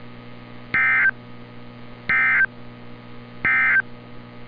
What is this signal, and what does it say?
Emergency Alert System (SAME) tones, heard as harsh buzzing bursts. It is heard from 0.6 to 3.9 s.
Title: NOAA Weather Radio - Altoona, PA